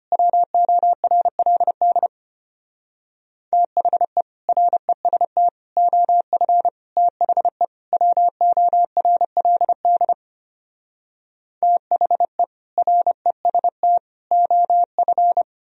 Morse code, heard as a keyed tone, about 33 wpm, 700 Hz. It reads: WORLD THE REST OF THE WORLD THE REST OF